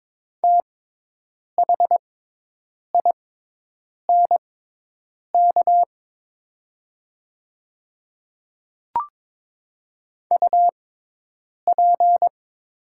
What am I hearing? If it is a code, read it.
Morse code, heard as a keyed tone, about 22 wpm, spaced as for 8 wpm, 700 Hz. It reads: THINK UP
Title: 6x Character Spacing - Top 200 Words - 22wpm